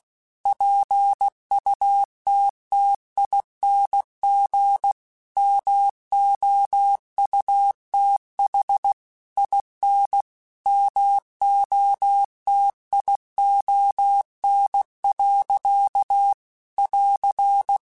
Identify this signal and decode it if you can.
Morse code, heard as a keyed tone, about 16 wpm, 800 Hz. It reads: PUTTING MOUTH IN MOTION. <AR>
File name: Chuckles / Chuckles-mp3s / BrainMouth @15